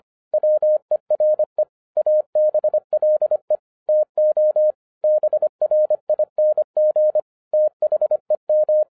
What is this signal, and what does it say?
Morse code, heard as a keyed tone, about 25 wpm, 610 Hz. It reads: WERE ABLE TO BRING THEM
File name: 2012-01-mp3 / n14-FUN WITH NUMBERS-25wpm